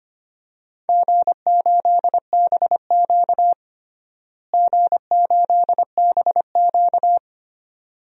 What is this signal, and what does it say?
Morse code, heard as a keyed tone, about 25 wpm, 700 Hz. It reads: G8BQ G8BQ
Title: DXCC-100 Call Signs - 4 Characters - Repeated 2x - 25wpm